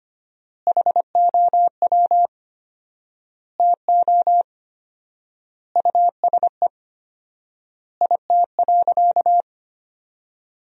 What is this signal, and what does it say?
Morse code, heard as a keyed tone, about 25 wpm, 700 Hz. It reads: HOW TO USE IT.
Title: 4x Word-Spacing - Sentences from Top 500 Words - 25wpm